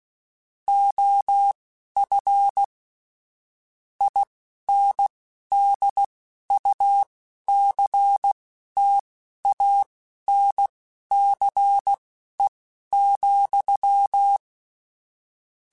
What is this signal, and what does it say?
Morse code, heard as a keyed tone, about 16 wpm, spaced as for 11 wpm, 800 Hz. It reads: OF INDUCTANCE,
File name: Technical Subjects / Technical Subjects-mp3s / XL @10